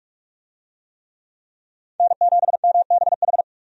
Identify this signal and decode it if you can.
Morse code, heard as a keyed tone, about 46 wpm, 700 Hz. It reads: N7MBH
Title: USA Call Signs - 5 Characters - 45wpm